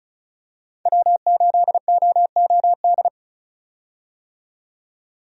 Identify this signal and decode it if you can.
Morse code, heard as a keyed tone, about 34 wpm, 700 Hz. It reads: W8OOD
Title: USA Call Signs - 5 Characters - 35wpm